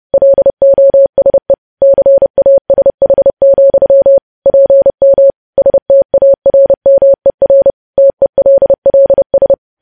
Morse code, heard as a keyed tone, about 30 wpm, 550 Hz. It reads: LOSE CASH, PM STARMER TELLS
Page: news headlines in morse code at 30 wpm